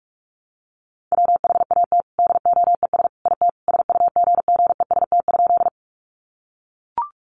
Morse code, heard as a keyed tone, about 44 wpm, 700 Hz. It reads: WHAT DOES IT SUGGEST?
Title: Sentences from Top 1300 Words 45wpm